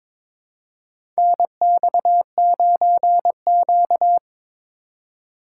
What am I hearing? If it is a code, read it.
Morse code, heard as a keyed tone, about 22 wpm, 700 Hz. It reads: NX9Q
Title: USA Call Signs - 6x Character Spacing - 22wpm